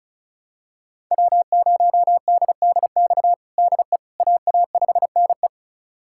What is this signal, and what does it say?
Morse code, heard as a keyed tone, about 36 wpm, 700 Hz. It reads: W0DDX DE AA5NE
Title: USA Callsign DE Callsign - 5 Characters - 35wpm